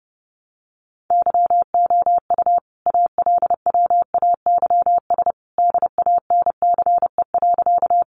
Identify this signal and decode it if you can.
Morse code, heard as a keyed tone, about 30 wpm, 700 Hz. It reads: YOU ALWAYS DANCE.
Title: Sentences from Top 700 Words 30wpm